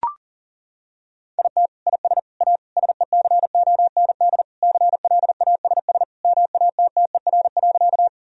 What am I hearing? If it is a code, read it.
Morse code, heard as a keyed tone, about 40 wpm, 700 Hz. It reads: IT IS A SECOND CLASS MATTER.
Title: Sentences from Top 1000 Words - Review Part 2 - 40wpm